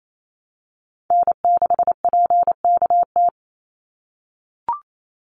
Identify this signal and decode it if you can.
Morse code, heard as a keyed tone, about 28 wpm, 700 Hz. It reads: N6PKT
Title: NA SOTA Chaser Callsigns 28wpm